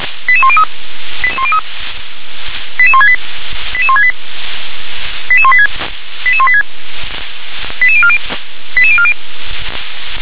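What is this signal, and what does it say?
Five-tone selective call (ZVEI), heard as quick runs of five tones, tones of about 70 ms. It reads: 88103 88168 88168 88186 88186 88040 88040
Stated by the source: Five-tone ZVEI selective call 88006 received by SDR.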